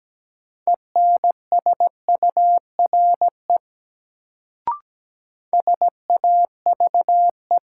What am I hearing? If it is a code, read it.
Morse code, heard as a keyed tone, about 17 wpm, 700 Hz. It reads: ENSURE SAVE
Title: Sentences from Top 1100 Words - New Word Drill Alt - 17wpm